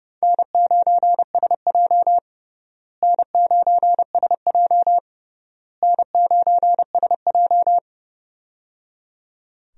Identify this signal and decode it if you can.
Morse code, heard as a keyed tone, about 30 wpm, 700 Hz. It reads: N9SJ N9SJ N9SJ
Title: USA Call Signs - 4 Characters - Repeated 3x - 30wpm